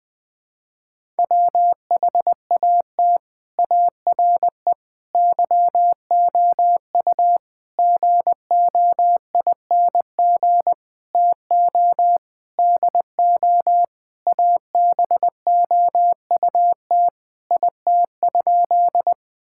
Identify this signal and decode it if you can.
Morse code, heard as a keyed tone, about 20 wpm, 700 Hz. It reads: WHAT ARE YOU GOING TO DO ABOUT IT?
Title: Sentences with 6-Word Phrases - 20wpm